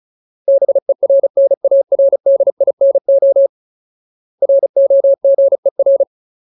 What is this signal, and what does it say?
Morse code, heard as a keyed tone, about 35 wpm, 550 Hz. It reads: BERNARDINO ROGER
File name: Ultimate Word List / 35WPM Ultimate